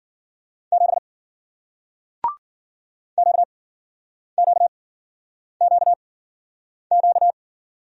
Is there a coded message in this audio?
Yes: Morse code from 0.7 to 7.3 s, heard as a keyed tone, about 56 wpm, spaced as for 9 wpm, 700 Hz. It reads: 7 QQQQ